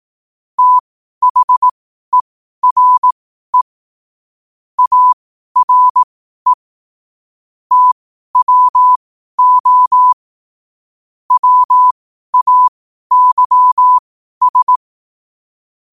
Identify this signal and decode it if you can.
Morse code, heard as a keyed tone, about 18 wpm, spaced as for 12 wpm, 1000 Hz. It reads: THERE ARE TWO WAYS
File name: qotdfiles / 12wpm